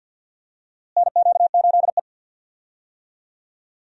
Morse code, heard as a keyed tone, about 49 wpm, 700 Hz. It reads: NQ8E
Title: USA Call Signs - 4 Characters - Repeated 2x - 50wpm